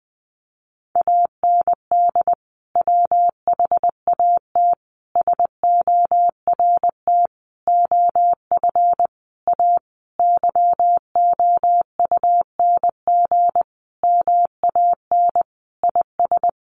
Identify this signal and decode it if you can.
Morse code, heard as a keyed tone, about 20 wpm, 700 Hz. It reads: AND WHAT SORT OF A YOUNG MAN IS
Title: Sentences from Top 1300 Words 20wpm